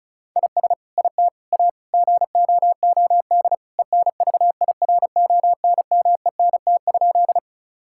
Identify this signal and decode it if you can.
Morse code, heard as a keyed tone, about 36 wpm, 700 Hz. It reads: IS IT A GOOD ENVIRONMENT?
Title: Sentences from Top 1000 Words - Review Part 5 - 35wpm